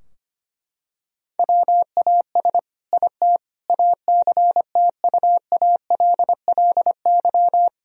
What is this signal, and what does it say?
Morse code, heard as a keyed tone, about 25 wpm, 700 Hz. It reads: WAS IT ACTUALLY
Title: Sentences from Top 1000 Words - Review Part 4 - 25wpm